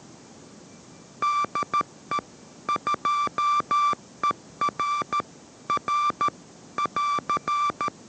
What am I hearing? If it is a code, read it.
Morse code, heard as a keyed tone, about 13 wpm, 1170 Hz. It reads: DE 2ER R <AR>
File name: 14wpm